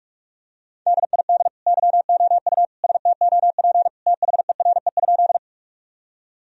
Morse code, heard as a keyed tone, about 46 wpm, 700 Hz. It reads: DID YOU STOP THERE?
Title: Sentences from Top 1000 Words - Review Part 1 - 45wpm